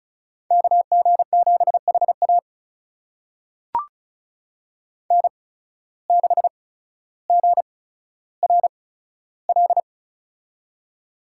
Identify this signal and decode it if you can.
Morse code, heard as a keyed tone, about 36 wpm, spaced as for 15 wpm, 700 Hz. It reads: KG7HA N6GRL
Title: USA Call Signs - 5 Characters - 8x Character Spacing - 35wpm